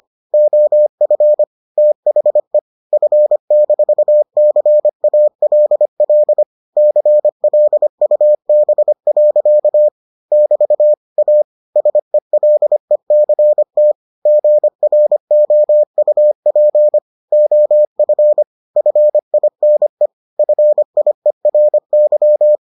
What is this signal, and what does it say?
Morse code, heard as a keyed tone, about 25 wpm, 600 Hz. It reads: OF THE F-CALL CLUB. <BT> A SELECT GROUP OF FINE FIERY